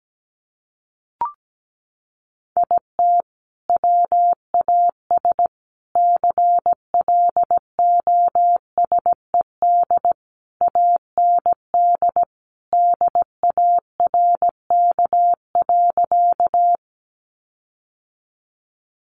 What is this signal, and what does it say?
Morse code, heard as a keyed tone, about 17 wpm, 700 Hz. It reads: IT WAS CLOSED AND DARK.
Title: Sentences from Top 1400 Words 17wpm